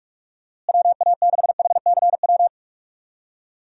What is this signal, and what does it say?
Morse code, heard as a keyed tone, about 47 wpm, 700 Hz. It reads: WA6HCW
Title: USA Call Signs - 6 Characters - Repeated 2x - 45wpm